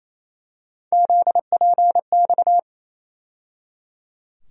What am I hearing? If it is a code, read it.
Morse code, heard as a keyed tone, about 28 wpm, 700 Hz. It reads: ZPX
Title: USA Call Sign Postfixes - 3 Letters 28wpm